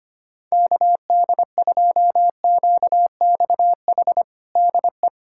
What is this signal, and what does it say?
Morse code, heard as a keyed tone, about 25 wpm, 700 Hz. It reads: KD2QXH DE
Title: USA Callsign DE Callsign - 6 Characters - 25wpm